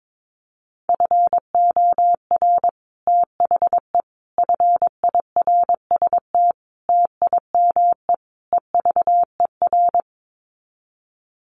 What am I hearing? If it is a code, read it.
Morse code, heard as a keyed tone, about 22 wpm, 700 Hz. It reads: FOR THE FIRST TIME EVER
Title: Sets of 5 Words Encore2 22wpm